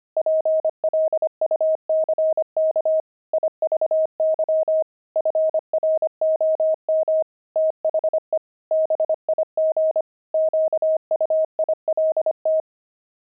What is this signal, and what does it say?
Morse code, heard as a keyed tone, about 25 wpm, 630 Hz. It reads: PLUCK IVY FROM THE BIG QUILT
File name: pangram-fs-25-cs-25-jinxed